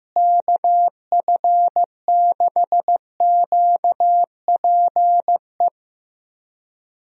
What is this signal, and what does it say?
Morse code, heard as a keyed tone, about 15 wpm, 700 Hz. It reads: KF6QPE
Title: USA Call Signs - 6 Characters - Repeated 3x - 15wpm